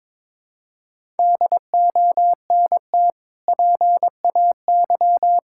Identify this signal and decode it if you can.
Morse code, heard as a keyed tone, about 21 wpm, 700 Hz. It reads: DONT PAY
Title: Sentences from Top 1000 Words - Review Part 6 - 22wpm